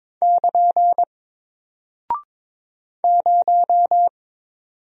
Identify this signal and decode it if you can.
Morse code, heard as a keyed tone, about 22 wpm, 700 Hz. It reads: <KN> 0
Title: Single Character (Letter:Number:Punctuation:Prosign) Alt 22wpm